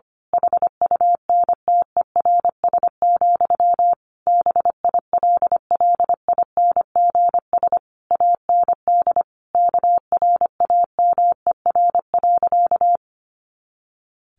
Morse code, heard as a keyed tone, about 25 wpm, 700 Hz. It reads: HUNTERS, BILLINGS AND KRAMER.
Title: A Sound Of Thunder 25wpm